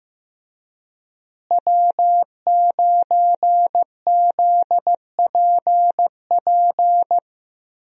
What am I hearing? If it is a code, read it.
Morse code, heard as a keyed tone, about 15 wpm, 700 Hz. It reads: W9ZPP
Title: USA Call Signs - 5 Characters - Repeated 3x - 15wpm